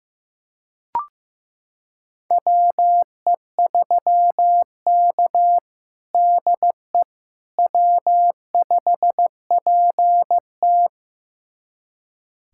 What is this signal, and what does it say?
Morse code, heard as a keyed tone, about 15 wpm, 700 Hz. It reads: WE3K DE W5PT
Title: USA Callsign DE Callsign - 4 Characters - 15wpm